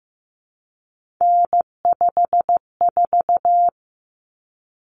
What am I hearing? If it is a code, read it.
Morse code, heard as a keyed tone, about 15 wpm, 700 Hz. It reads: N54
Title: Common Cut Numbers - 3-Digit 15wpm